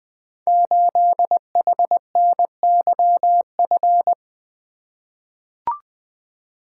Morse code, heard as a keyed tone, about 20 wpm, 700 Hz. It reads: 8HNYF
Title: Code Groups - 5 Letter:Numbers - 20wpm